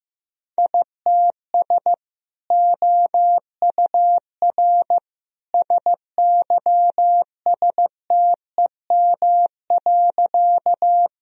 Morse code, heard as a keyed tone, about 15 wpm, 700 Hz. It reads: ITS OUR SYSTEM.
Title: Sentences from Top 1000 Words - Review Part 3 - 15wpm